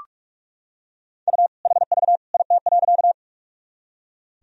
Morse code, heard as a keyed tone, about 46 wpm, 700 Hz. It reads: U HV IT.
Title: Sentences from Top 100 Words and Abbreviations 45wpm